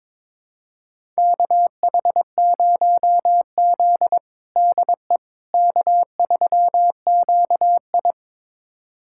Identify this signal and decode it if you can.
Morse code, heard as a keyed tone, about 21 wpm, 700 Hz. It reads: KH0Z DE K3QI
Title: USA Callsign DE Callsign - 4 Characters - 22wpm